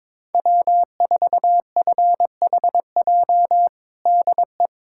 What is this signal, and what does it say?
Morse code, heard as a keyed tone, about 22 wpm, 700 Hz. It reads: W4FHJ DE
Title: USA Callsign DE Callsign - 5 Characters - 22wpm